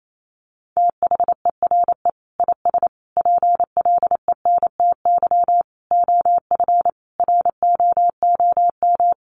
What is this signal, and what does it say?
Morse code, heard as a keyed tone, about 28 wpm, 700 Hz. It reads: THERE IS PLENTY OF ROOM
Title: Sentences from Top 1300 Words 28wpm